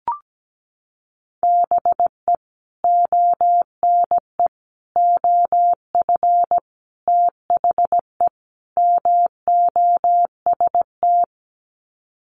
Morse code, heard as a keyed tone, about 17 wpm, 700 Hz. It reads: BE ONE OF THE MOST
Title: Sets of 5 Words Encore 17wpm